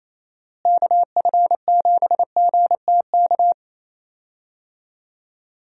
Morse code, heard as a keyed tone, about 27 wpm, 700 Hz. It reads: KF7GTK
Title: USA Call Signs - 6 Characters - Repeated 2x - 28wpm